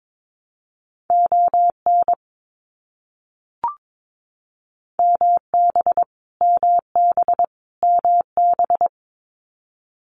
Morse code, heard as a keyed tone, about 22 wpm, 700 Hz. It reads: ON MB MB MB
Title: Canadian Provinces and Territories - Repeated 3x - 22wpm